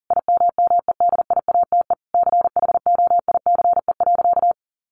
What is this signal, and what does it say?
Morse code, heard as a keyed tone, about 40 wpm, 700 Hz. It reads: IMMEDIATE CHOICE.
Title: Sentences from Top 1200 Words 40wpm